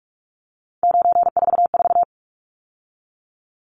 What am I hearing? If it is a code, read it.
Morse code, heard as a keyed tone, about 44 wpm, 700 Hz. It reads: 944
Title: Numbers - 3-Digit 45wpm